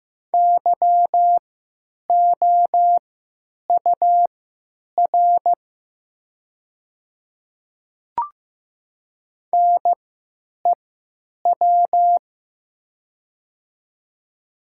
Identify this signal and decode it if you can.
Morse code, heard as a keyed tone, about 15 wpm, spaced as for 9 wpm, 700 Hz. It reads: YOUR NEW
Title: 3x Character Spacing - Top 300 Words - 15wpm